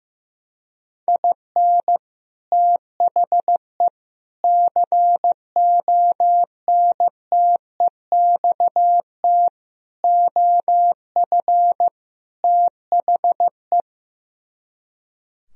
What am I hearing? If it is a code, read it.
Morse code, heard as a keyed tone, about 16 wpm, 700 Hz. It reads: IN THE CONTEXT OF THE
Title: Speed-Racing - Sets of 5 Words 15wpm